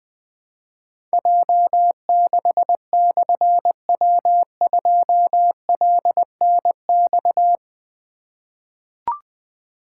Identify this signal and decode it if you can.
Morse code, heard as a keyed tone, about 20 wpm, 700 Hz. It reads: J6/W2LNX
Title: Difficult Call Signs 20wpm